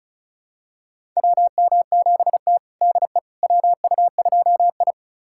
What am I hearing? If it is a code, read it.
Morse code, heard as a keyed tone, about 36 wpm, 700 Hz. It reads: WM7T DE WU2I
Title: USA Callsign DE Callsign - 4 Characters - 35wpm